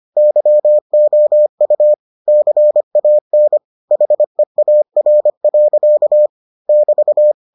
Morse code, heard as a keyed tone, about 25 wpm, 600 Hz. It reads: YOU CAN HEAR. <BT>